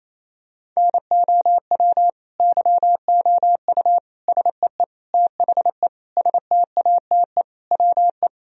Morse code, heard as a keyed tone, about 28 wpm, 700 Hz. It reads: NOW YOU SEE THE STATE WE
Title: Sentences from Top 1000 Words - Review Part 2 - 28wpm